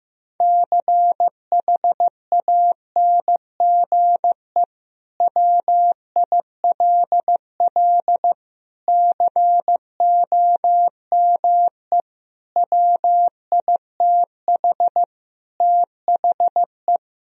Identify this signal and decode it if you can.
Morse code, heard as a keyed tone, about 15 wpm, 700 Hz. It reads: CHANGE WILL COME WITH THE
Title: Sentences from Top 1000 Words - Review Part 2 - 15wpm